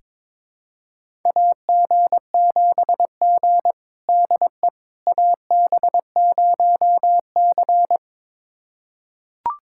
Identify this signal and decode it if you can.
Morse code, heard as a keyed tone, about 21 wpm, 700 Hz. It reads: AG7G DE AB0C
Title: USA Callsign DE Callsign - 4 Characters - 22wpm